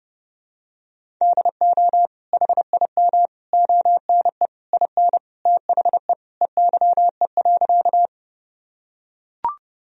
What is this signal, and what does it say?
Morse code, heard as a keyed tone, about 32 wpm, 700 Hz. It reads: DO HIM ONE IN THE EYE.
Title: Sentences from Top 1000 Words - Review Part 1 - 30wpm